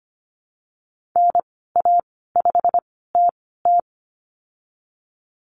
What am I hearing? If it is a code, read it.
Morse code, heard as a keyed tone, about 25 wpm, spaced as for 16 wpm, 700 Hz. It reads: NA5TT
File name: Lesson 15g - Number 5 - Call Signs 25wpm